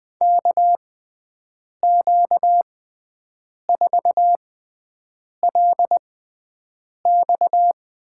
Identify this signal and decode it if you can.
Morse code, heard as a keyed tone, about 20 wpm, spaced as for 7 wpm, 700 Hz. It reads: KQ4LX
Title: USA Call Signs - 5 Characters - 6x Character Spacing - 20wpm